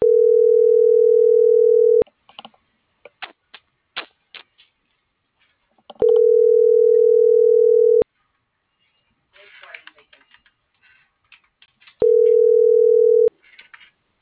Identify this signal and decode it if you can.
Ringback tone (North America), 440 + 480 Hz.